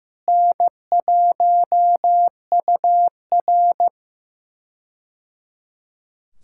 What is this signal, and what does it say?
Morse code, heard as a keyed tone, about 15 wpm, 700 Hz. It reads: N1UR
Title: K1USN SST QSO Practice - 15wpm